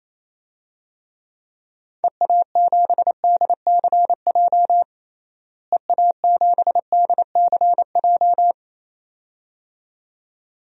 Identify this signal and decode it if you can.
Morse code, heard as a keyed tone, about 28 wpm, 700 Hz. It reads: EA7DCJ EA7DCJ
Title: DXCC-100 Call Signs - 6 Characters - Repeated 2x - 28wpm